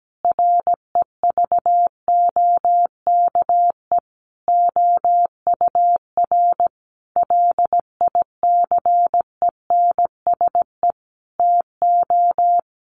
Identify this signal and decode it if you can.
Morse code, heard as a keyed tone, about 17 wpm, 700 Hz. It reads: REVOKE OUR LICENSE TO
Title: A Sound Of Thunder 17wpm